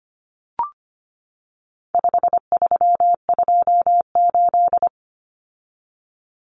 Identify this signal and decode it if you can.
Morse code, heard as a keyed tone, about 25 wpm, 700 Hz. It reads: 5328
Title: Numbers - 4-Digit 25wpm